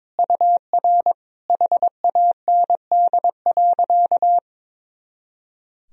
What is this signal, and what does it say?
Morse code, heard as a keyed tone, about 22 wpm, 700 Hz. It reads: UR HAND.
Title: Sentences from Top 400 Words and Abbreviations 22wpm